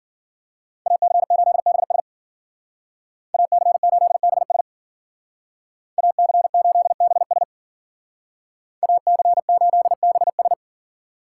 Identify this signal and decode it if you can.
Morse code, heard as a keyed tone, about 50 wpm, 700 Hz. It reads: AC8BS AC8BS AC8BS AC8BS